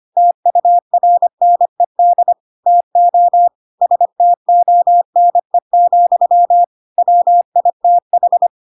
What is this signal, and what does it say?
Morse code, heard as a keyed tone, about 25 wpm, 700 Hz. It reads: TURNED TO STONE, WITH
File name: Pinocchio / Pinocchio ch5 25 25 ik1hge